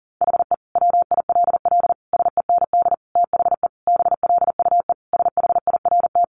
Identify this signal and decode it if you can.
Morse code, heard as a keyed tone, about 40 wpm, 700 Hz. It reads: HE WILL SEND THE BLUE SHIRT
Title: Additional Easy Phrases 40wpm